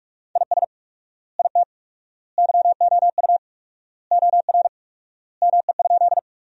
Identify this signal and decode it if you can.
Morse code, heard as a keyed tone, about 46 wpm, 700 Hz. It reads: IS IT YOU OR ME?
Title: 4x Word-Spacing - Sentences from Top 200 Words - 45wpm